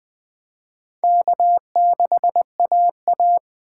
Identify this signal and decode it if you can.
Morse code, heard as a keyed tone, about 20 wpm, 700 Hz. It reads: K6AA